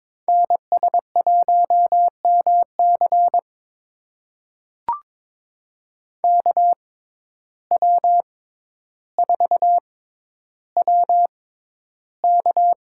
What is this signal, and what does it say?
Morse code, heard as a keyed tone, about 22 wpm, spaced as for 11 wpm, 700 Hz. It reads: NS1MC KW4WK
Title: USA Call Signs - 5 Characters - 6x Character Spacing - 22wpm